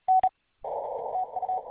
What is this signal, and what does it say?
Morse code, heard as a keyed tone, about 32 wpm, 750 Hz. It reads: N TU